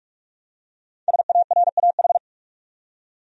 Morse code, heard as a keyed tone, about 46 wpm, 700 Hz. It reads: SARAH
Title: USA Common Names - Max 5 Letters - 45wpm